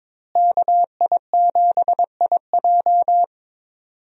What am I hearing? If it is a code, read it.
Morse code, heard as a keyed tone, about 22 wpm, 700 Hz. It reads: KI7IJ
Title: USA Call Signs - 5 Characters - Repeated 3x - 22wpm